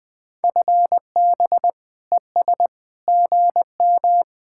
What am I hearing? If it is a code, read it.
Morse code, heard as a keyed tone, about 22 wpm, 700 Hz. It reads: FB ES GM
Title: Speed-Racing Common QSO Phrases - 20wpm